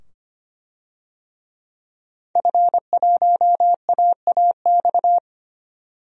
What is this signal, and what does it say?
Morse code, heard as a keyed tone, about 25 wpm, 700 Hz. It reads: F1AAX
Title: DXCC-100 Call Signs - 5 Characters - 25wpm